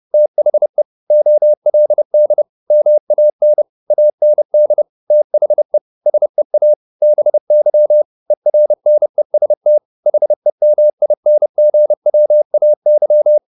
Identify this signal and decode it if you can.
Morse code, heard as a keyed tone, about 30 wpm, 600 Hz. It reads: THE OLD MAN AND THE SEA BY ERNEST HEMINGWAY
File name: the old man and the sea 2 30wpm